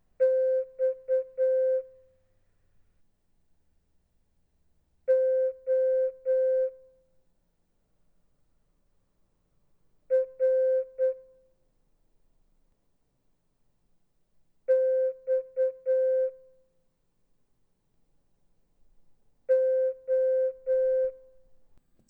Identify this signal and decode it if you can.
Morse code, heard as a keyed tone, about 8 wpm, 520 Hz. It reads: X O R X O
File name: Morse-Gul-oppgave